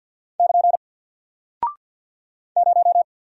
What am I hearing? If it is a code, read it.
Morse code, heard as a keyed tone, about 50 wpm, 700 Hz. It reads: <KN> 0
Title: Single Character (Letter:Number:Punctuation:Prosign) Alt 50wpm